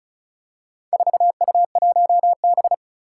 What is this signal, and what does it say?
Morse code, heard as a keyed tone, about 36 wpm, 700 Hz. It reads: 4U1B
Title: Four Challenging Letter-Number Combinations 35wpm